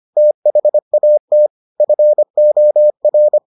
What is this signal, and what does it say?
Morse code, heard as a keyed tone, about 25 wpm, 600 Hz. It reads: THAT FOR